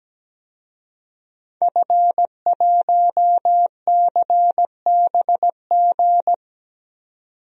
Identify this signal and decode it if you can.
Morse code, heard as a keyed tone, about 17 wpm, 700 Hz. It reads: F1CBG
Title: DXCC-100 Call Signs - 5 Characters - 17wpm